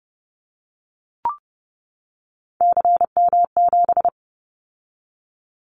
Morse code, heard as a keyed tone, about 30 wpm, 700 Hz. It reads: CM7
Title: Callsign Prefixes for Top 100 DXCC - 30wpm